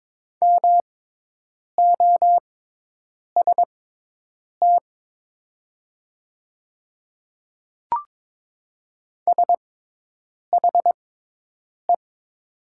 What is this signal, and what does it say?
Morse code, heard as a keyed tone, about 22 wpm, spaced as for 8 wpm, 700 Hz. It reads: MOST SHE
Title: 6x Character Spacing - Top 100 Words - 22wpm